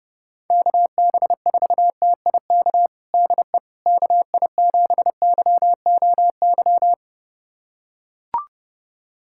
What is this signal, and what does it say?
Morse code, heard as a keyed tone, about 30 wpm, 700 Hz. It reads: KB4TIK DE KI7YOY
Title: USA Callsign DE Callsign - 6 Characters - 30wpm